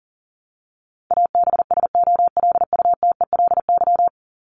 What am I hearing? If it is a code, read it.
Morse code, heard as a keyed tone, about 40 wpm, 700 Hz. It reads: ABSOLUTELY
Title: LY Suffix - Top 100 Words with Base Words 40wpm